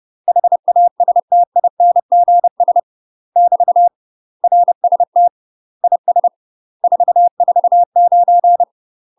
Morse code, heard as a keyed tone, about 30 wpm, 700 Hz. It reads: HASTINGS <BT> RST IS 449
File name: UK MASTER QSOs TRAINING / 30WPM / UK MASTER QSOs 0039